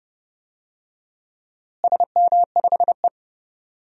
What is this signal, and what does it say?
Morse code, heard as a keyed tone, about 30 wpm, 700 Hz. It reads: SM5E
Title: DXCC-100 Call Signs - 4 Characters - Repeated 2x - 30wpm